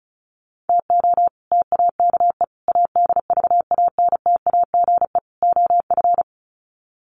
Morse code, heard as a keyed tone, about 34 wpm, 700 Hz. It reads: TO TAKE ADVANTAGE OF
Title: Sets of 4 Words 35wpm